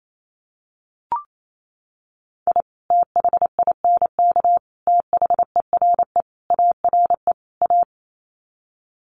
Morse code, heard as a keyed tone, about 28 wpm, 700 Hz. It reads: I THINK THERE ARE A
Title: Sets of 5 Words Encore2 28wpm